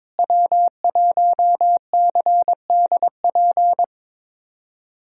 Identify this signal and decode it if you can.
Morse code, heard as a keyed tone, about 22 wpm, 700 Hz. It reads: W1CDP
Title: USA Callsign DE Callsign 22wpm